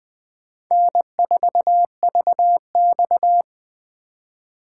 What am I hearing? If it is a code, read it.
Morse code, heard as a keyed tone, about 20 wpm, 700 Hz. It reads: N4VX
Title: USA Call Signs - 4 Characters - 2x Character Spacing - 20wpm